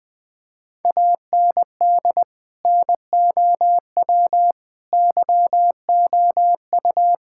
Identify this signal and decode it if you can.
Morse code, heard as a keyed tone, about 20 wpm, 700 Hz. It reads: AND NOW YOU
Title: Sentences from Top 1000 Words - Review Part 6 - 20wpm